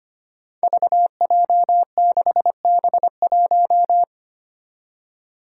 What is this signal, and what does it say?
Morse code, heard as a keyed tone, about 25 wpm, 700 Hz. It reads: VJ6B1
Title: Five Challenging Letter-Number Combinations 25wpm